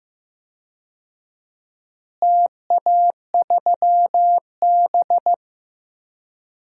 Morse code, heard as a keyed tone, about 15 wpm, 700 Hz. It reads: TA3B
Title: DXCC-100 Call Signs - 4 Characters - Repeated 3x - 15wpm